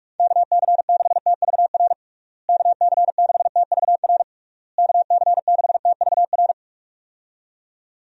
Morse code, heard as a keyed tone, about 46 wpm, 700 Hz. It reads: KC6TVR KC6TVR KC6TVR
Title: USA Call Signs - 6 Characters - Repeated 3x - 45wpm